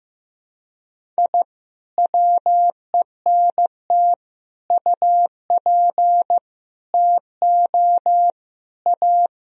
Morse code, heard as a keyed tone, about 15 wpm, 700 Hz. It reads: I WENT UP TO A
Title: Sentences from Top 1000 Words - Review Part 1 - 15wpm